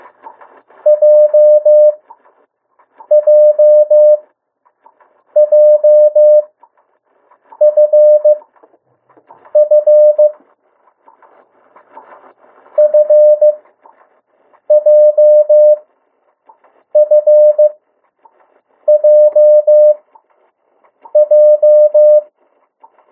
Morse code, heard as a keyed tone, about 15 wpm, spaced as for 6 wpm, 600 Hz. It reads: JJJFF FJFJJ
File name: urok 1